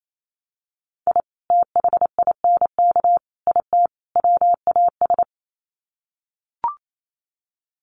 Morse code, heard as a keyed tone, about 28 wpm, 700 Hz. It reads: I THINK IT WAS
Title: Sets of 4 Words 28wpm